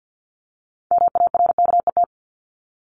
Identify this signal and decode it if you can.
Morse code, heard as a keyed tone, about 50 wpm, 700 Hz. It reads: MARKET
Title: Top 501-750 Words 50wpm